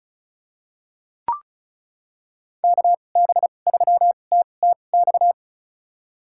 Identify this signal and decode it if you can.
Morse code, heard as a keyed tone, about 36 wpm, spaced as for 25 wpm, 700 Hz. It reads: KB3TTX
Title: USA Call Signs - 6 Characters - 2x Character Spacing - 35wpm